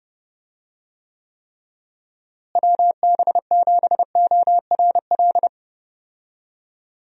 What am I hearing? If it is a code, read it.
Morse code, heard as a keyed tone, about 30 wpm, 700 Hz. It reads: WB7ORL
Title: USA Call Signs - 6 Characters - Repeated 3x - 30wpm